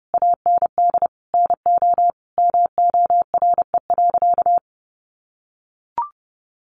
Morse code, heard as a keyed tone, about 28 wpm, 700 Hz. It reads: AND NO MORE.
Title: Sentences from Top 1100 Words 30wpm